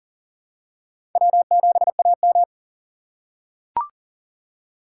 Morse code, heard as a keyed tone, about 40 wpm, 700 Hz. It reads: W7AM
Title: USA Call Signs - 4 Characters - 2x Character Spacing - 40wpm